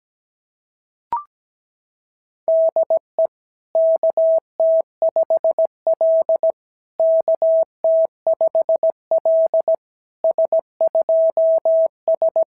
Morse code, heard as a keyed tone, about 16 wpm, 650 Hz. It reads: DE KT5L KT5L S2S
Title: SOTA QSO Practice 17wpm